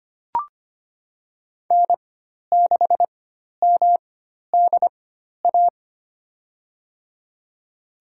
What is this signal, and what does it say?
Morse code, heard as a keyed tone, about 25 wpm, spaced as for 12 wpm, 700 Hz. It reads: N6MDA
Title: USA Call Signs - 5 Characters - 4x Character Spacing - 25wpm